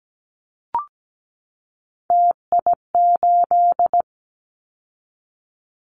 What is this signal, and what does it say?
Morse code, heard as a keyed tone, about 17 wpm, 700 Hz. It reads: TI8
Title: Callsign Prefixes for Top 100 DXCC - 17wpm